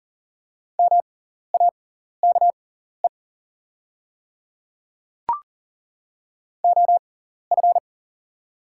Morse code, heard as a keyed tone, about 38 wpm, spaced as for 14 wpm, 690 Hz. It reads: MAKE OF
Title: 6x Character Spacing - Top 100 Words - 40wpm